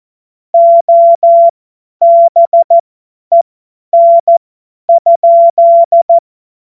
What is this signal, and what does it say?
Morse code, heard as a keyed tone, about 14 wpm, spaced as for 10 wpm, 680 Hz. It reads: OBEN?
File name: Morsekurs / Witze / Witz 02. Februar 2021, 50 F70